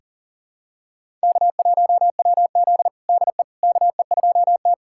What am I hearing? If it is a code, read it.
Morse code, heard as a keyed tone, about 40 wpm, 700 Hz. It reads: K1WZ DE KE2T
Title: USA Callsign DE Callsign - 4 Characters - 40wpm